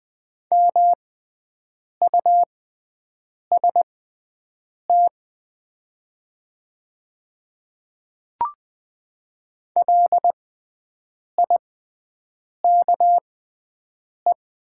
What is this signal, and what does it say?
Morse code, heard as a keyed tone, about 20 wpm, spaced as for 7 wpm, 700 Hz. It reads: MUST LIKE
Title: 6x Character Spacing - Top 200 Words - 20wpm